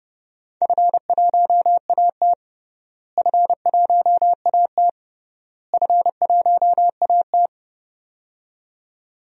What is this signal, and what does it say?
Morse code, heard as a keyed tone, about 31 wpm, 700 Hz. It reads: F1AT F1AT F1AT
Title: DXCC-100 Call Signs - 4 Characters - Repeated 3x - 30wpm